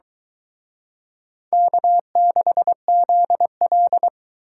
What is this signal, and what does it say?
Morse code, heard as a keyed tone, about 23 wpm, 700 Hz. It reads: K6ZL